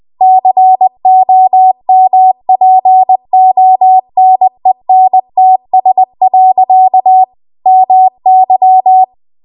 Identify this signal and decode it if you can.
Morse code, heard as a keyed tone, about 20 wpm, 750 Hz. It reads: COMPONENTS. MY
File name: Archive / 20 WPM / 20WPM